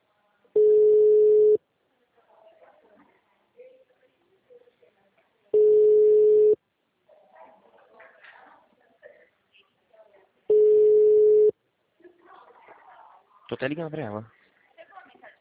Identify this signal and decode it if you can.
Ringback tone (Europe, 425 Hz), with speech.